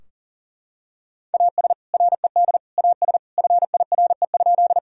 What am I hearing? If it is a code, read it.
Morse code, heard as a keyed tone, about 37 wpm, 700 Hz. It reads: AS RED AS FIRE?
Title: Sentences from Top 1000 Words - Review Part 2 - 40wpm